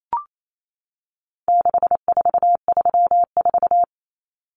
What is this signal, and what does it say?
Morse code, heard as a keyed tone, about 28 wpm, 700 Hz. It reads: 6434
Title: Numbers - 4-Digit 28wpm